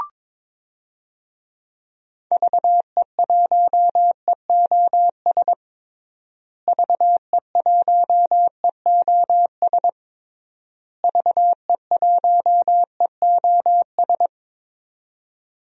Morse code, heard as a keyed tone, about 22 wpm, 700 Hz. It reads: VE1EOS VE1EOS VE1EOS
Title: DXCC-100 Call Signs - 6 Characters - Repeated 3x - 22wpm